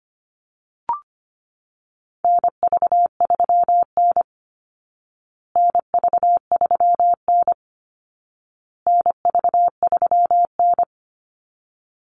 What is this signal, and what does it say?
Morse code, heard as a keyed tone, about 25 wpm, 700 Hz. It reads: NV3N NV3N NV3N
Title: Call Signs Repeated 2x - 25wpm